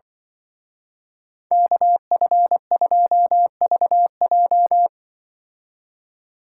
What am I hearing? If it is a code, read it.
Morse code, heard as a keyed tone, about 23 wpm, 700 Hz. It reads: KF2VJ
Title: Speed Racing - USA 2x2 Call Signs 20wpm